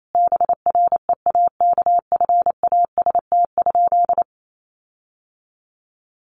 Morse code, heard as a keyed tone, about 28 wpm, 700 Hz. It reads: BREAKFAST?
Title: Sentences from Top 1500 Words 28wpm